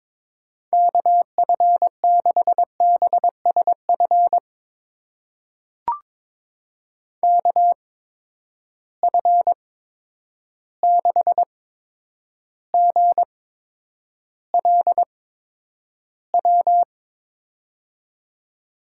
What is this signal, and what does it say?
Morse code, heard as a keyed tone, about 22 wpm, spaced as for 9 wpm, 700 Hz. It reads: KF6BSF KF6GLW
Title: USA Call Signs - 6 Characters - 8x Character Spacing - 22wpm